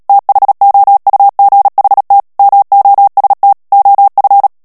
Morse code, heard as a keyed tone, about 37 wpm, 800 Hz. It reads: THOUGHT MOST OF
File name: morse mp3 / ANGLAIS texte clair 2 vitesse 35 mots minute